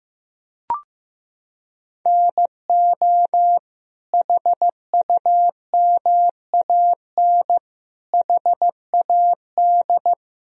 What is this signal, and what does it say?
Morse code, heard as a keyed tone, about 15 wpm, 700 Hz. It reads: NO HUMAN HAD
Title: Sentences from Top 1000 Words - Review Part 3 - 15wpm